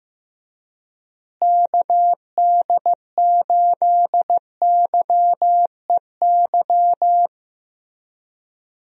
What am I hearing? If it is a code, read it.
Morse code, heard as a keyed tone, about 15 wpm, 700 Hz. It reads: KD8YEY
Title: USA Call Signs - 6 Characters - Repeated 2x - 15wpm